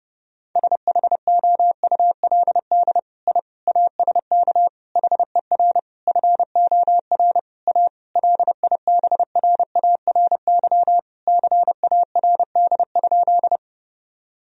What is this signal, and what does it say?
Morse code, heard as a keyed tone, about 32 wpm, 700 Hz. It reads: SHOULD I ASK HER FOR A LIBRARY CARD?
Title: Sentences from Top 1000 Words - Review Part 4 - 30wpm